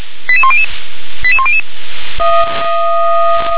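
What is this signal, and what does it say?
Five-tone selective call (ZVEI), heard as quick runs of five tones, tones of about 70 ms. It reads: 88100 88100
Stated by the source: ZVEI five-tone selective call received on an SDR.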